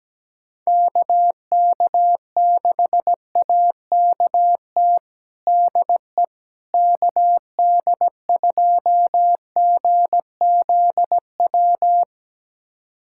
Morse code, heard as a keyed tone, about 17 wpm, 700 Hz. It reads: KK6AKT DE KD2GZW
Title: USA Callsign DE Callsign - 6 Characters - 17wpm